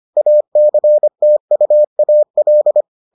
Morse code, heard as a keyed tone, about 25 wpm, 600 Hz. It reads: ACTUAL